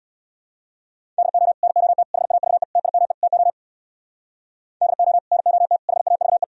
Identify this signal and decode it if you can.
Morse code, heard as a keyed tone, about 63 wpm, 700 Hz. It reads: DO NOT HAVE TIME TO DO NOT HAVE